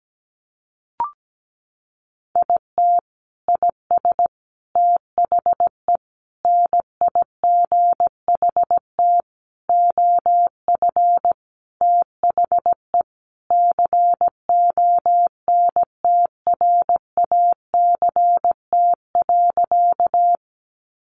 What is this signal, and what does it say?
Morse code, heard as a keyed tone, about 17 wpm, 700 Hz. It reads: IT IS THE NIGHT OF THE CONTRACT.
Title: Sentences from Top 1400 Words 17wpm